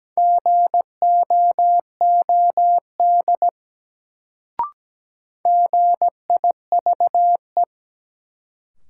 Morse code, heard as a keyed tone, about 17 wpm, 700 Hz. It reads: GOOD GIVE
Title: Sentences from Top 200 Words - New Word Drill - 17wpm